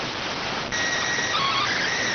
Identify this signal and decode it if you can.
SSTV (slow-scan television), PD 290.